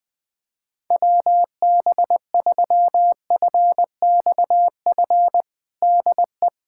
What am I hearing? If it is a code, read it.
Morse code, heard as a keyed tone, about 20 wpm, 700 Hz. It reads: WB3FXF DE
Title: USA Callsign DE Callsign 20wpm